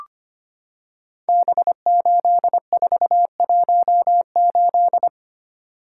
Morse code, heard as a keyed tone, about 25 wpm, 700 Hz. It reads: B8418
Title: Code Groups - 5 Letter:Numbers - 25wpm